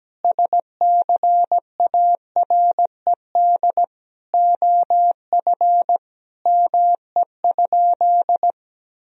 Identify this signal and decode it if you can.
Morse code, heard as a keyed tone, about 17 wpm, 700 Hz. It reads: SCARED OF ME?
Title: Sentences from Top 1000 Words - Review Part 4 - 17wpm